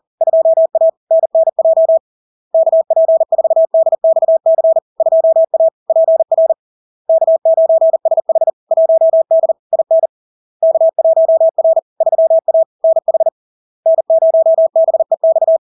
Morse code, heard as a keyed tone, about 40 wpm, 650 Hz. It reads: 2A NNJ KP4DXC 2A PR K9SH 1D IN K1R 3A NH N0BEX
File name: LARK-FD-2022-Log-in-CW-40wpm-650-Hz